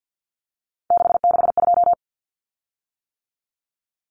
Morse code, heard as a keyed tone, about 48 wpm, 700 Hz. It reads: 662
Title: Common Cut Numbers - 3-Digit 50wpm